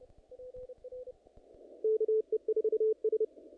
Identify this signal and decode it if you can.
Morse code, heard as a keyed tone, about 30 wpm, 440 Hz. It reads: KE4S
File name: contests / naqp-cw-k5tr-op-k5pi- -2000